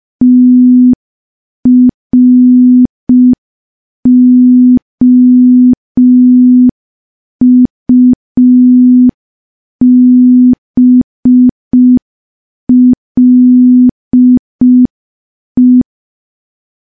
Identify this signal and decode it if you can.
Morse code, heard as a keyed tone, about 5 wpm, 250 Hz. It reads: TROUBLE
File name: morse / morse-training-05WPM